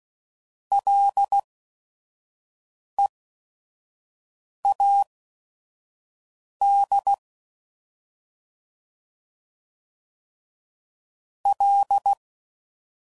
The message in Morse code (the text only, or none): LEAD L